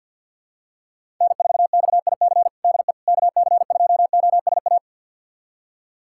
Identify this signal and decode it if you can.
Morse code, heard as a keyed tone, about 46 wpm, 700 Hz. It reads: N4XIC DE KC2OSA
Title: USA Callsign DE Callsign 50wpm